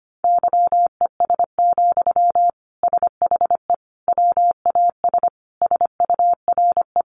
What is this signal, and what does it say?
Morse code, heard as a keyed tone, about 25 wpm, 690 Hz. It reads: YES, SHE WAS SURE
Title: Sentences from Top 1100 Words 25wpm